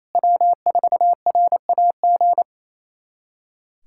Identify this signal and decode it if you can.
Morse code, heard as a keyed tone, about 28 wpm, 700 Hz. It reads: W4RAG
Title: USA Callsign DE Callsign - 5 Characters - 28wpm